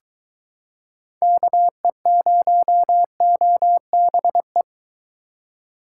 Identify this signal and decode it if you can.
Morse code, heard as a keyed tone, about 23 wpm, 700 Hz. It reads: KE0OBE